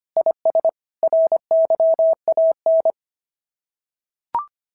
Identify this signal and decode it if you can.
Morse code, heard as a keyed tone, about 22 wpm, 650 Hz. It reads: IS RYAN
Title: SOTA QSO Practice 25wpm